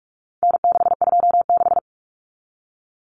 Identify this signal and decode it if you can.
Morse code, heard as a keyed tone, about 44 wpm, 700 Hz. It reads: N626
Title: Common Cut Numbers - 4-Digit 45wpm